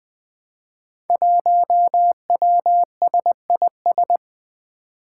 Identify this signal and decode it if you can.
Morse code, heard as a keyed tone, about 20 wpm, 700 Hz. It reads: 1WSIS
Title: Five Challenging Letter-Number Combinations 20wpm